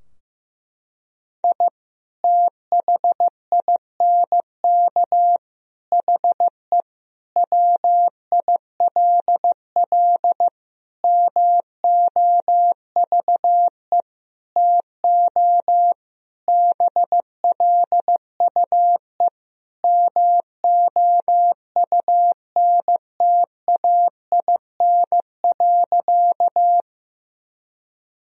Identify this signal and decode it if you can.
Morse code, heard as a keyed tone, about 15 wpm, 700 Hz. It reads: I THINK HE WILL MOVE TO BLUE MOUNTAIN.
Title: Sentences from Top 1000 Words - Review Part 3 - 15wpm